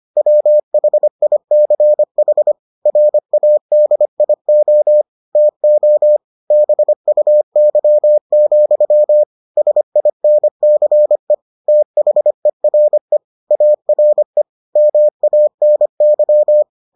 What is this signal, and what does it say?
Morse code, heard as a keyed tone, about 25 wpm, 600 Hz. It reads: WHICH RADIO TO BUY, SINCE THERE ARE MANY